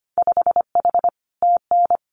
Morse code, heard as a keyed tone, about 25 wpm, 700 Hz. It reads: 5H TN
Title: Winter Field Day Exchanges Encore 25wpm